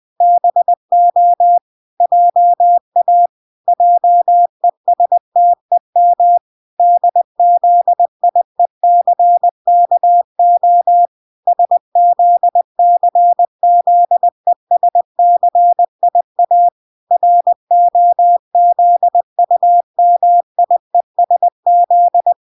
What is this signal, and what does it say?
Morse code, heard as a keyed tone, about 20 wpm, 700 Hz. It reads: BO JA JESTEM DZIECKO SZCZESCIA ROZUMIESZ